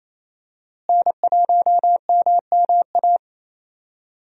Morse code, heard as a keyed tone, about 28 wpm, 700 Hz. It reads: N1MMA
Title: USA Call Signs - 5 Characters - Repeated 2x - 28wpm